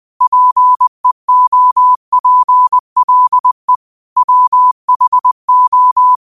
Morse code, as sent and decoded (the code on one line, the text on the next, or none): .--. --- .--. .-.. / .-- .... ---
PEOPLE WHO